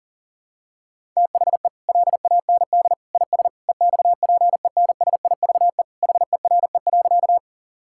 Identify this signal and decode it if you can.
Morse code, heard as a keyed tone, about 40 wpm, 700 Hz. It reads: THE LAND IS EXPENSIVE HERE.
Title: Sentences from Top 1000 Words - Review Part 6 - 40wpm